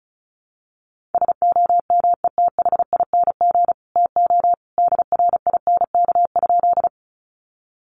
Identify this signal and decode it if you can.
Morse code, heard as a keyed tone, about 34 wpm, 700 Hz. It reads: SOMETHING TO DRINK?
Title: Sentences from Top 1500 Words 35wpm